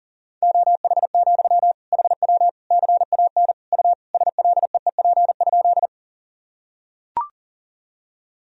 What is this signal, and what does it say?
Morse code, heard as a keyed tone, about 40 wpm, 700 Hz. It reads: OH, HW CAN U SLEEP?
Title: Sentences from Top 500 Words and Abbreviations 40wpm